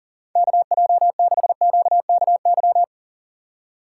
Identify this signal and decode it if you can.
Morse code, heard as a keyed tone, about 40 wpm, 690 Hz. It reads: KJ6QKY
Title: USA Call Signs - 8x Character Spacing - 40wpm